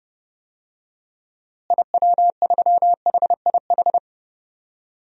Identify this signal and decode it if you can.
Morse code, heard as a keyed tone, about 30 wpm, 700 Hz. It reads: IW3HIH
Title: DXCC-100 Call Signs - 6 Characters - Repeated 2x - 30wpm